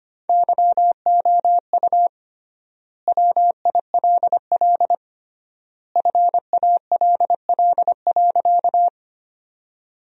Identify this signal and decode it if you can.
Morse code, heard as a keyed tone, about 25 wpm, 700 Hz. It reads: YOU WILL FALL.
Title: 3x Word-Spacing - Sentences from Top 500 Words - 25wpm